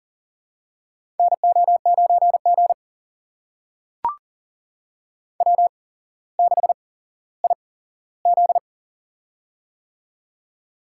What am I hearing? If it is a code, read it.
Morse code, heard as a keyed tone, about 40 wpm, spaced as for 17 wpm, 700 Hz. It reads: NO9G W6IZ
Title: USA Call Signs - 8x Character Spacing - 40wpm